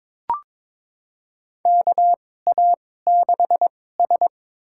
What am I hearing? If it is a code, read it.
Morse code, heard as a keyed tone, about 22 wpm, spaced as for 16 wpm, 700 Hz. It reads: KA6S
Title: USA Call Signs - 2x Character Spacing - 22wpm